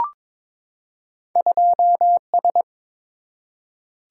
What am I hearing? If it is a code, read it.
Morse code, heard as a keyed tone, about 22 wpm, 700 Hz. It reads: 2S
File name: practice / 22wpm / Two Letter-Number Combinations - 22wpm